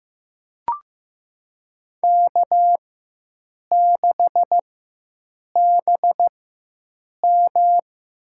Morse code, heard as a keyed tone, about 15 wpm, spaced as for 7 wpm, 700 Hz. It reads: K6BM
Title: USA Call Signs - 5 Characters - 4x Character Spacing - 15wpm